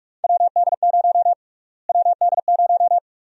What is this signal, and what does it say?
Morse code, heard as a keyed tone, about 44 wpm, 700 Hz. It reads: WD0 WD0
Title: USA Call Sign Prefixes - Repeated 2x - 45wpm